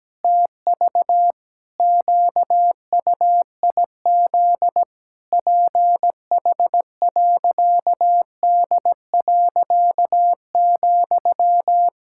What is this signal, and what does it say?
Morse code, heard as a keyed tone, about 17 wpm, 700 Hz. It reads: TV QUIZ PH.D.,
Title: Pangram Challenge v2 - 17wpm